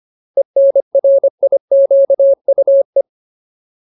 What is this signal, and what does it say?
Morse code, heard as a keyed tone, about 25 wpm, 550 Hz. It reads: ENRIQUE
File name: Ultimate Word List / 25WPM Ultimate